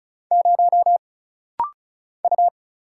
Morse code, heard as a keyed tone, about 34 wpm, 700 Hz. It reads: 0 U
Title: Single Letter-Number Alt - Rapid-Fire 35wpm